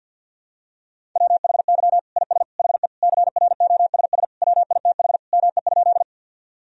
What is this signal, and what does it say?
Morse code, heard as a keyed tone, about 51 wpm, 700 Hz. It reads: WHY IS HE CROSS WITH ME?
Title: Sentences from Top 1000 Words - Review Part 1 - 50wpm